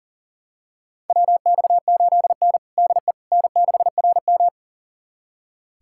Morse code, heard as a keyed tone, about 40 wpm, 690 Hz. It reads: WX8N DE N6RM
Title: USA Callsign DE Callsign - 4 Characters - 40wpm